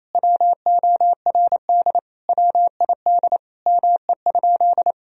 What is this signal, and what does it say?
Morse code, heard as a keyed tone, about 28 wpm, 700 Hz. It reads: WORD WID ME?
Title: Sentences from Top 300 Words and Abbreviations 28wpm